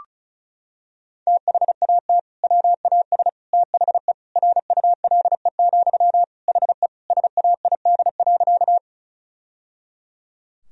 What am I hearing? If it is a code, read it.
Morse code, heard as a keyed tone, about 34 wpm, 700 Hz. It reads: THAT WAS THE RULE, HE SAID.
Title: Sentences from Top 1000 Words - Review Part 2 - 35wpm